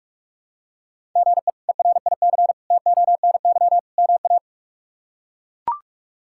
Morse code, heard as a keyed tone, about 41 wpm, 700 Hz. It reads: GE ERIC TONY MA
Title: K1USN SST QSO Practice - 45wpm